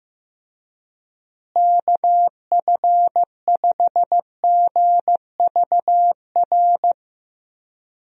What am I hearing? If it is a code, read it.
Morse code, heard as a keyed tone, about 16 wpm, 700 Hz. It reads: KF5GVR
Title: USA Call Signs - 6 Characters - Repeated 3x - 15wpm